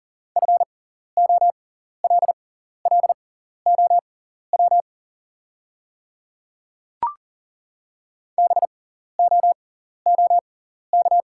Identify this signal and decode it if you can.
Morse code, heard as a keyed tone, about 40 wpm, spaced as for 14 wpm, 690 Hz. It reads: FOLLOW BOOK
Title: 6x Character Spacing - Top 300 Words - 40wpm